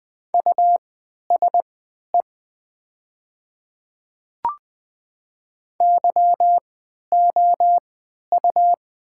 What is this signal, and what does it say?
Morse code, heard as a keyed tone, about 20 wpm, spaced as for 11 wpm, 700 Hz. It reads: USE YOU
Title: 3x Character Spacing - Top 100 Words - 20wpm